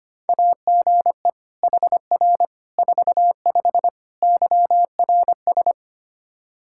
Morse code, heard as a keyed tone, about 25 wpm, 700 Hz. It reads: AGE HR 45 YRS
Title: K1USN SST QSO Practice 25wpm